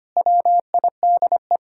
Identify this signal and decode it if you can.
Morse code, heard as a keyed tone, about 25 wpm, 700 Hz. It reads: WIDE
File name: practice / 25wpm / Sentences with 4-Word Phrases - 25wpm